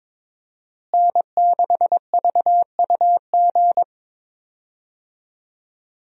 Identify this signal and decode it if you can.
Morse code, heard as a keyed tone, about 22 wpm, 700 Hz. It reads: N6VUG
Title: USA Call Signs - 5 Characters - Repeated 2x - 22wpm